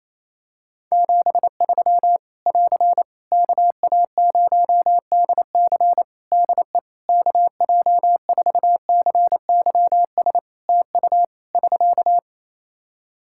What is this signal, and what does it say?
Morse code, heard as a keyed tone, about 27 wpm, 700 Hz. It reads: 73 <AR> KA0DC DE KJ4CYS TU <SK>
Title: K1USN SST QSO Practice 28wpm